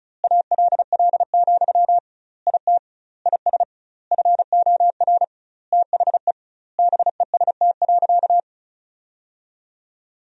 Morse code, heard as a keyed tone, about 34 wpm, 700 Hz. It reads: ALL, IT IS FOR THE BEST.
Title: 2x Word-Spacing - Sentences from Top 500 Words - 35wpm